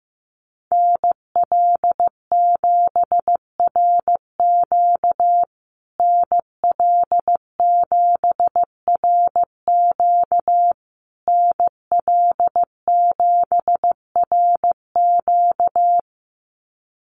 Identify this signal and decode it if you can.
Morse code, heard as a keyed tone, about 15 wpm, 700 Hz. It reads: NL7RQ NL7RQ NL7RQ
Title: USA Call Signs Repeated 3x 15wpm